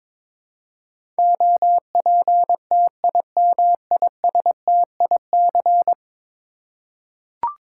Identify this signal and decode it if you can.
Morse code, heard as a keyed tone, about 22 wpm, 700 Hz. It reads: OPTIMISTIC
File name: practice / 22wpm / 10 Letter Words - First 3 Letters Given - 22wpm